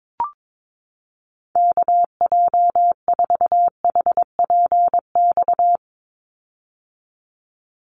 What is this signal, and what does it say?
Morse code, heard as a keyed tone, about 22 wpm, 700 Hz. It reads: KJ4HPX
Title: USA Call Signs 22wpm